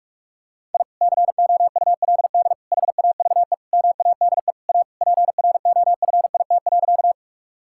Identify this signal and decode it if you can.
Morse code, heard as a keyed tone, about 47 wpm, 700 Hz. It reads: I COULD HAVE MADE A PROFIT.
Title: Sentences from Top 1000 Words - Review Part 5 - 45wpm